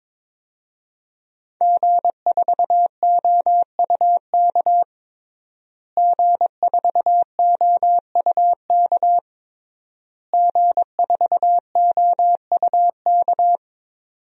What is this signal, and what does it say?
Morse code, heard as a keyed tone, about 22 wpm, 700 Hz. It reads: G4OUK G4OUK G4OUK
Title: DXCC-100 Call Signs - 5 Characters - Repeated 3x - 22wpm